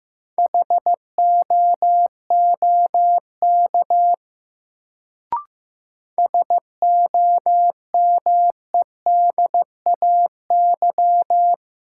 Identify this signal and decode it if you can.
Morse code, heard as a keyed tone, about 15 wpm, 700 Hz. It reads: HOOK SOMEDAY
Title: Sentences from Top 1600 Words - New Word Drill - 15wpm